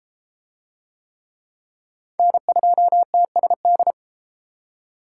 Morse code, heard as a keyed tone, about 32 wpm, 700 Hz. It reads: N2TSD